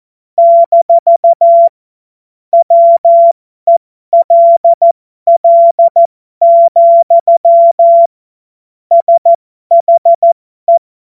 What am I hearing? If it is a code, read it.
Morse code, heard as a keyed tone, about 14 wpm, spaced as for 12 wpm, 680 Hz. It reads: - WELL, SHE
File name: Morsekurs / Witze